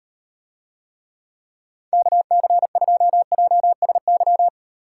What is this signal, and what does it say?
Morse code, heard as a keyed tone, about 38 wpm, 700 Hz. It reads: KC2JSY